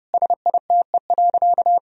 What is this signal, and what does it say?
Morse code, heard as a keyed tone, about 30 wpm, 700 Hz. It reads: SITE.
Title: Sentences from Top 1000 Words - Review Part 4 - 30wpm